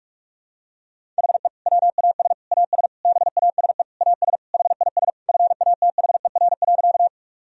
Morse code, heard as a keyed tone, about 46 wpm, 700 Hz. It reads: HE WAS AS BASE AS HIS FATHER.
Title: Sentences from Top 1000 Words - Review Part 1 - 45wpm